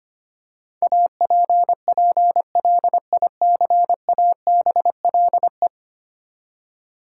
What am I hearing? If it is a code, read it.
Morse code, heard as a keyed tone, about 25 wpm, 700 Hz. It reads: APPLICABLE
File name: Ten Letter Words - 25wpm